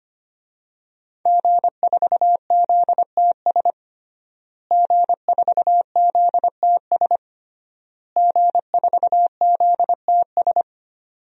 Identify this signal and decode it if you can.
Morse code, heard as a keyed tone, about 25 wpm, 700 Hz. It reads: G4ZTS G4ZTS G4ZTS
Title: DXCC-100 Call Signs - 5 Characters - Repeated 3x - 25wpm